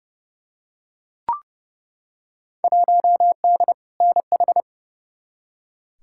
Morse code, heard as a keyed tone, about 30 wpm, 700 Hz. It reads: 1D NH
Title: Field Day Exchanges - 30wpm